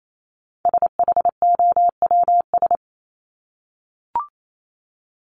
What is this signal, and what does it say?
Morse code, heard as a keyed tone, about 28 wpm, 700 Hz. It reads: SHOWS
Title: ES:S Suffix - Top 100 Words with Base Words 28wpm